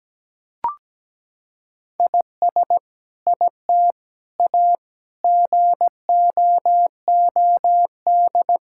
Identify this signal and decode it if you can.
Morse code, heard as a keyed tone, about 17 wpm, 700 Hz. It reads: IS IT A GOOD
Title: Sentences from Top 1000 Words - Review Part 5 - 17wpm